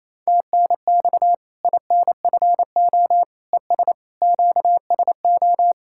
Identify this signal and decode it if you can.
Morse code, heard as a keyed tone, about 28 wpm, 700 Hz. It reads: TNX INFO ES QSO
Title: Practice QSO - Exchange 4 - Ending - 28wpm